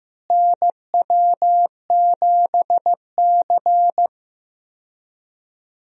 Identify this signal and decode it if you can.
Morse code, heard as a keyed tone, about 16 wpm, 700 Hz. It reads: NW7C
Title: USA Call Signs - 4 Characters - Repeated 3x - 15wpm